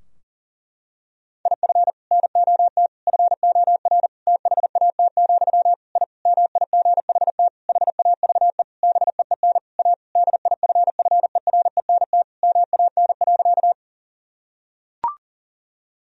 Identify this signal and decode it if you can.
Morse code, heard as a keyed tone, about 37 wpm, 700 Hz. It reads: IF NOT FOR THAT, I MIGHT HAVE BEEN A DIFFERENT MAN.
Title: Sentences from Top 1000 Words - Review Part 3 - 40wpm